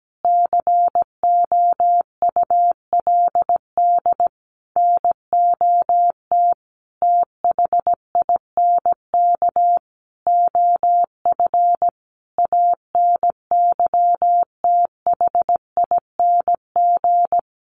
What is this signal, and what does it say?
Morse code, heard as a keyed tone, about 17 wpm, 700 Hz. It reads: COULD NOT THINK OF ANYTHING
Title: Sentences from Top 1000 Words 17wpm